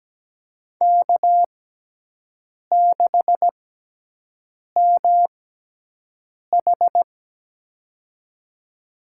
Morse code, heard as a keyed tone, about 17 wpm, spaced as for 6 wpm, 700 Hz. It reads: K6MH
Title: USA Call Signs - 4 Characters - 6x Character Spacing - 17wpm